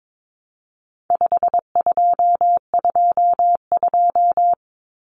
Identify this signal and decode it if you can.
Morse code, heard as a keyed tone, about 22 wpm, 700 Hz. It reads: 5222
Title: Numbers - 4-Digit 22wpm